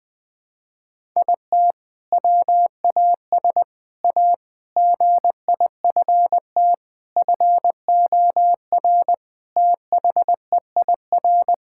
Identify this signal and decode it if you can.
Morse code, heard as a keyed tone, about 20 wpm, 700 Hz. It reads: IT WAS A GIFT FOR THEIR
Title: Sentences from Top 1000 Words - Review Part 6 - 20wpm